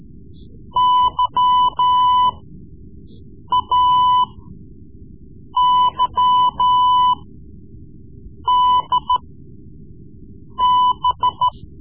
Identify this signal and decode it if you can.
Morse code, heard as a keyed tone, about 14 wpm, spaced as for 5 wpm, 1000 Hz. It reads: YAYDB
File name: Morse Lessons / Morse Lessons / Tape-3B